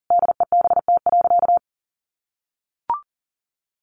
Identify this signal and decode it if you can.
Morse code, heard as a keyed tone, about 40 wpm, 700 Hz. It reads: DEBT.
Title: Sentences from Top 1300 Words 40wpm